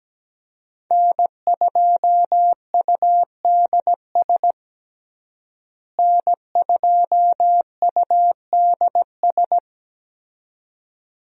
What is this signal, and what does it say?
Morse code, heard as a keyed tone, about 17 wpm, 700 Hz. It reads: N2UDS N2UDS
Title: USA Call Signs - 5 Characters - Repeated 2x - 17wpm